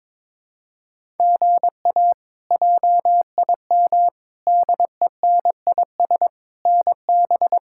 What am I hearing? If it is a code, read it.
Morse code, heard as a keyed tone, about 21 wpm, 700 Hz. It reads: GA JIM DENIS NB
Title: K1USN SST QSO Practice - 22wpm